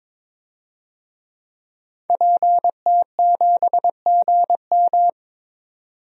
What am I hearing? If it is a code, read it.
Morse code, heard as a keyed tone, about 21 wpm, 700 Hz. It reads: PT7GM
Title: DXCC-100 Call Signs - 5 Characters - 22wpm